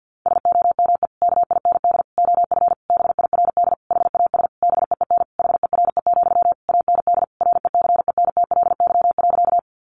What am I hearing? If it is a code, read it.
Morse code, heard as a keyed tone, about 49 wpm, 700 Hz. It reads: SOME KIND OF BIRD HAS BEEN HERE, AND RECENTLY.
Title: Sentences from Top 1200 Words 50wpm